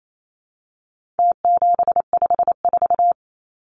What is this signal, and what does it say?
Morse code, heard as a keyed tone, about 28 wpm, 700 Hz. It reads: T754
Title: Numbers - Contest Style - 4-Digit 28wpm